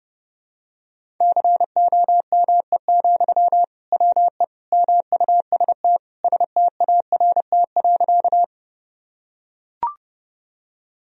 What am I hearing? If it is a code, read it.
Morse code, heard as a keyed tone, about 30 wpm, 700 Hz. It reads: COME, WE MUST START.
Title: Sentences from Top 1000 Words - Review Part 1 - 30wpm